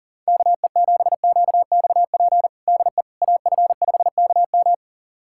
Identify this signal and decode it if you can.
Morse code, heard as a keyed tone, about 40 wpm, 700 Hz. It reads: KE7QXP DE AF5KM
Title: USA Callsign DE Callsign 40wpm